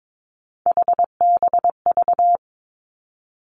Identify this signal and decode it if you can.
Morse code, heard as a keyed tone, about 22 wpm, 700 Hz. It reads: HBV
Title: Three Challenging Letter-Number Combinations 22wpm